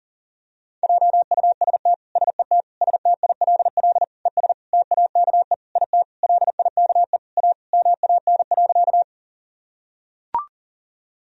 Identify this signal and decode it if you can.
Morse code, heard as a keyed tone, about 39 wpm, 700 Hz. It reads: JUST SET STILL ES TAKE IT LIKE A MAN.
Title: Sentences from Top 300 Words and Abbreviations 40wpm